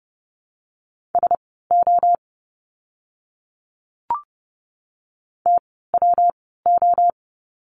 Morse code, heard as a keyed tone, about 30 wpm, spaced as for 17 wpm, 700 Hz. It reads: SO TWO
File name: course / 30:10wpm / 230 Lesson 37f - Letter x - Words 30wpm